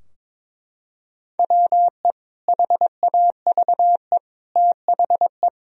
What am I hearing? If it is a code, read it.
Morse code, heard as a keyed tone, about 22 wpm, 700 Hz. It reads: WE HAVE THE
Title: Sentences from Top 1000 Words - Review Part 3 - 22wpm